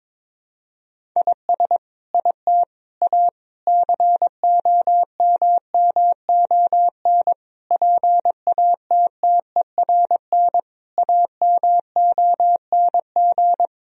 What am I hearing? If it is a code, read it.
Morse code, heard as a keyed tone, about 22 wpm, 700 Hz. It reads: IS IT A COMMON PATTERN AMONG
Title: Sentences from Top 1000 Words - Review Part 2 - 22wpm